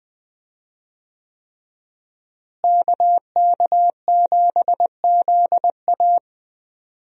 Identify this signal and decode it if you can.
Morse code, heard as a keyed tone, about 20 wpm, 700 Hz. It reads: KK7ZA
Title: USA Call Signs - 5 Characters - Repeated 3x - 20wpm